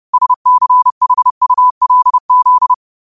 Morse code, heard as a keyed tone, about 30 wpm, 1000 Hz. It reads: SCHULZ